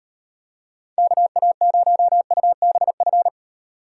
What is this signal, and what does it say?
Morse code, heard as a keyed tone, about 38 wpm, 700 Hz. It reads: KA0UBF